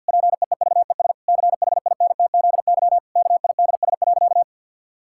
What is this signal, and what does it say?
Morse code, heard as a keyed tone, about 51 wpm, 700 Hz. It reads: PEEVES CHINTZY KIDS.
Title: Pangram Challenge v2 - 50wpm